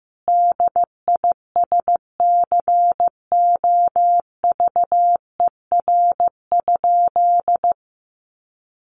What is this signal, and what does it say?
Morse code, heard as a keyed tone, about 15 wpm, 700 Hz. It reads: DISCOVER?
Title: Sentences from Top 1100 Words 15wpm